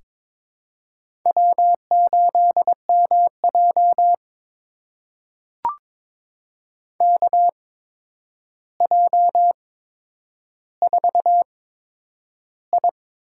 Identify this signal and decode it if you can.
Morse code, heard as a keyed tone, about 22 wpm, spaced as for 9 wpm, 700 Hz. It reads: W8MJ KJ4I
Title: USA Call Signs - 8x Character Spacing - 22wpm